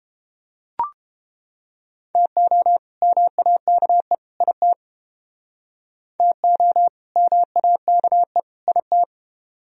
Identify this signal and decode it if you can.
Morse code, heard as a keyed tone, about 29 wpm, 700 Hz. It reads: TO MAKE IT TO MAKE IT